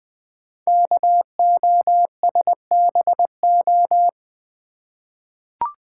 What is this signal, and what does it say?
Morse code, heard as a keyed tone, about 20 wpm, 700 Hz. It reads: KOSBO
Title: Code Groups - 5 Letters - 20wpm